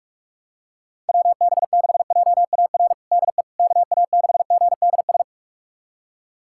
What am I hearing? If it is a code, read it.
Morse code, heard as a keyed tone, about 46 wpm, 700 Hz. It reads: WB6JAR DE KA6GDS
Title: USA Callsign DE Callsign - 6 Characters - 45wpm